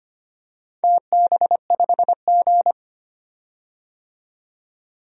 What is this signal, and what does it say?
Morse code, heard as a keyed tone, about 25 wpm, 700 Hz. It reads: TB5G
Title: DXCC-100 Call Signs - 4 Characters - 25wpm